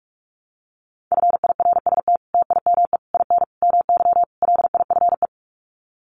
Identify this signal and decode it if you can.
Morse code, heard as a keyed tone, about 44 wpm, 700 Hz. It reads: FIRST TIME IN MY LIFE
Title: Sets of 5 Words Encore2 45wpm